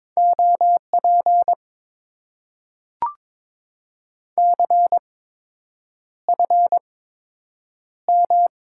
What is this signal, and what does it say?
Morse code, heard as a keyed tone, about 22 wpm, spaced as for 8 wpm, 700 Hz. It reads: OP CFM
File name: Top 90 QSO Elements - 8x Character Spacing - 22wpm